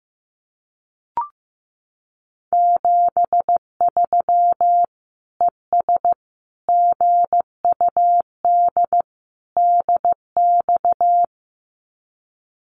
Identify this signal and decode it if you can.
Morse code, heard as a keyed tone, about 15 wpm, 700 Hz. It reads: 73 ES GUD DX
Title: Common QSO Phrases 15wpm